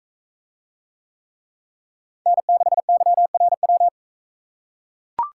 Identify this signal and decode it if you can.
Morse code, heard as a keyed tone, about 44 wpm, 700 Hz. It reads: N6YRW